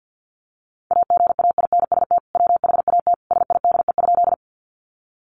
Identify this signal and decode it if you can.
Morse code, heard as a keyed tone, about 50 wpm, 700 Hz. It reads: AGAINST WHAT SIDE?
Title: Sentences from Top 400 Words 50wpm